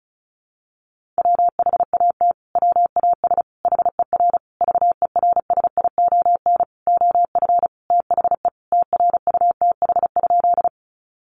Morse code, heard as a keyed tone, about 34 wpm, 700 Hz. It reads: WHAT WAS HER VERSION OF THE TRUTH?
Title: Sentences from Top 1400 Words 35wpm